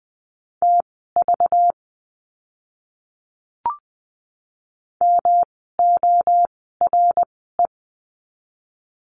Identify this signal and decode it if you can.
Morse code, heard as a keyed tone, about 20 wpm, spaced as for 15 wpm, 700 Hz. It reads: TV MORE
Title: 192 Lesson 31f - Letter v - Words 20wpm